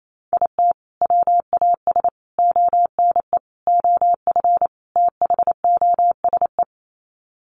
Morse code, heard as a keyed tone, about 28 wpm, 700 Hz. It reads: IT WAS ONE OF THOSE
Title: Sets of 5 Words Encore 28wpm